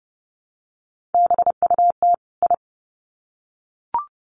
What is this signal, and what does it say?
Morse code, heard as a keyed tone, about 28 wpm, 700 Hz. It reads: BUT I
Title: Sets of 2 Words 30wpm